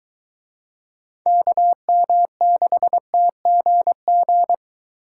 Morse code, heard as a keyed tone, about 23 wpm, 700 Hz. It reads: KM6TGG